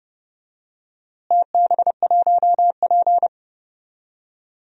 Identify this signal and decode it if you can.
Morse code, heard as a keyed tone, about 30 wpm, 700 Hz. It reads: TB1P
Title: DXCC-100 Call Signs - 4 Characters - Repeated 3x - 30wpm